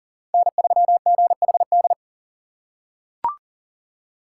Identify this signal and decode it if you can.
Morse code, heard as a keyed tone, about 40 wpm, 700 Hz. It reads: N3GHD
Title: USA Call Signs - 6x Character Spacing - 40wpm